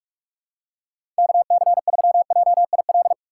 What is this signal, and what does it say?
Morse code, heard as a keyed tone, about 46 wpm, 700 Hz. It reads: KC3JIL
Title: USA Call Signs - 6 Characters - 2x Character Spacing - 45wpm